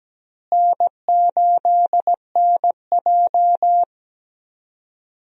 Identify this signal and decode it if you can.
Morse code, heard as a keyed tone, about 17 wpm, 700 Hz. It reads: N8NJ
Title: USA Call Signs - 4 Characters - Repeated 2x - 17wpm